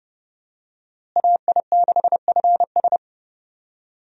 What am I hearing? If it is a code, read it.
Morse code, heard as a keyed tone, about 30 wpm, 700 Hz. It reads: AI6FS
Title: USA Call Signs - 5 Characters - Repeated 3x - 30wpm